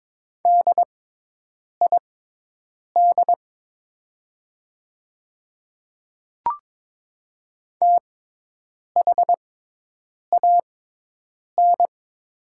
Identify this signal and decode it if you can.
Morse code, heard as a keyed tone, about 22 wpm, spaced as for 8 wpm, 700 Hz. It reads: DID THAN
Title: 6x Character Spacing - Top 200 Words - 22wpm